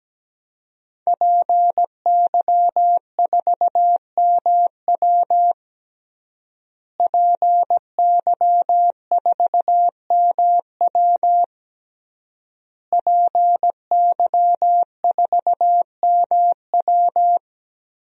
Morse code, heard as a keyed tone, about 17 wpm, 700 Hz. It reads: PY4MW PY4MW PY4MW
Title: DXCC-100 Call Signs - 5 Characters - Repeated 3x - 17wpm